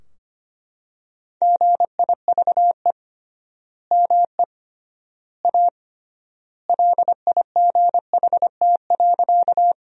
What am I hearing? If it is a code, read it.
Morse code, heard as a keyed tone, about 25 wpm, 700 Hz. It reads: GIVE ME A LIGHT.
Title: 3x Word-Spacing - Sentences from Top 200 Words - 25wpm